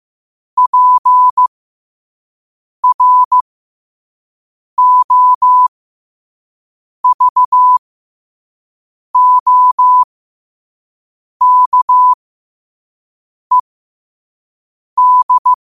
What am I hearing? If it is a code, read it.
Morse code, heard as a keyed tone, about 15 wpm, spaced as for 5 wpm, 1000 Hz. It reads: PROVOKED
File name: qotdfiles / 5wpm